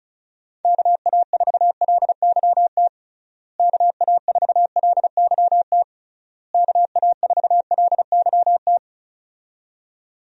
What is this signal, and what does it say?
Morse code, heard as a keyed tone, about 34 wpm, 700 Hz. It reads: KA4LYT KA4LYT KA4LYT
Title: USA Call Signs - 6 Characters - Repeated 3x - 35wpm